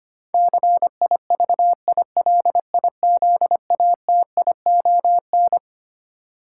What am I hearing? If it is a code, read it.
Morse code, heard as a keyed tone, about 25 wpm, 700 Hz. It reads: CIVILIZATION
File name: Twelve Letter Words - 25wpm